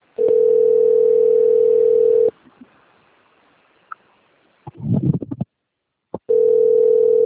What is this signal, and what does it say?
Ringback tone (North America), 440 + 480 Hz.